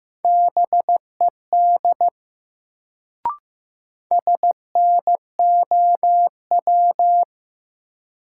Morse code, heard as a keyed tone, about 15 wpm, 700 Hz. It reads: BED SNOW
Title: Sentences from Top 500 Words - New Word Drill - 15wpm